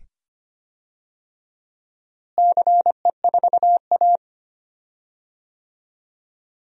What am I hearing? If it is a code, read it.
Morse code, heard as a keyed tone, about 25 wpm, 700 Hz. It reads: CE4A
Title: DXCC-100 Call Signs - 4 Characters - Repeated 2x - 25wpm